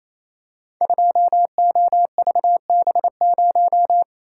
Morse code, heard as a keyed tone, about 28 wpm, 700 Hz. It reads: 2OVB0
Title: Code Groups - 5 Letter:Numbers - 28wpm